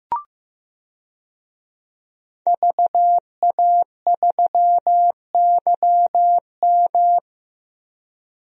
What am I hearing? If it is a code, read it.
Morse code, heard as a keyed tone, about 15 wpm, 700 Hz. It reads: VA3YM
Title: DXCC-100 Call Signs - 5 Characters - 15wpm